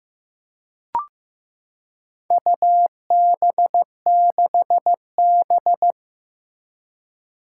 Morse code, heard as a keyed tone, about 15 wpm, 700 Hz. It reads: UB6B
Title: Four Challenging Letter-Number Combinations 15wpm